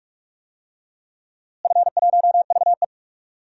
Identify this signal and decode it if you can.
Morse code, heard as a keyed tone, about 46 wpm, 700 Hz. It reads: F1VE
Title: DXCC-100 Call Signs - 4 Characters - Repeated 3x - 45wpm